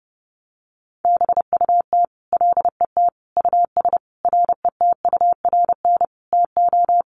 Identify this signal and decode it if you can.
Morse code, heard as a keyed tone, about 30 wpm, 700 Hz. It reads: BUT LET US RETURN TO
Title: Sentences from Top 800 Words 30wpm